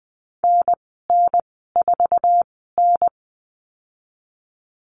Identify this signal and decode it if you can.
Morse code, heard as a keyed tone, about 20 wpm, spaced as for 14 wpm, 700 Hz. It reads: NN4N
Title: 048 Lesson 9g - Number 4 - Call Signs 20wpm